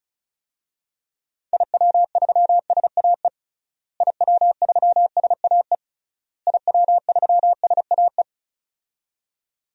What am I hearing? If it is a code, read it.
Morse code, heard as a keyed tone, about 36 wpm, 700 Hz. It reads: IW3SAE IW3SAE IW3SAE
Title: DXCC-100 Call Signs - 6 Characters - Repeated 3x - 35wpm